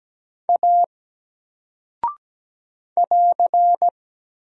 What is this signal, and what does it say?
Morse code, heard as a keyed tone, about 17 wpm, 700 Hz. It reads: A <AR>
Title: Single Character (Letter:Number:Punctuation:Prosign) 17wpm